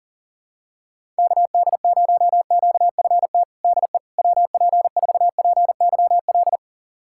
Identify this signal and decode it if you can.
Morse code, heard as a keyed tone, about 40 wpm, 700 Hz. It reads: KD0QFT DE WP4PYL
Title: USA Callsign DE Callsign - 6 Characters - 40wpm